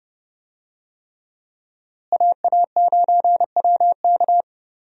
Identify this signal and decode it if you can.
Morse code, heard as a keyed tone, about 30 wpm, 700 Hz. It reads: AA9WK
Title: USA Call Signs - 5 Characters - Repeated 3x - 30wpm